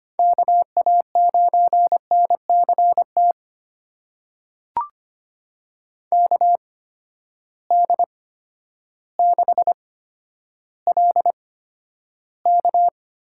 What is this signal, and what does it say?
Morse code, heard as a keyed tone, about 25 wpm, spaced as for 11 wpm, 700 Hz. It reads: KA9NCT KD6LK
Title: USA Call Signs - 8x Character Spacing - 25wpm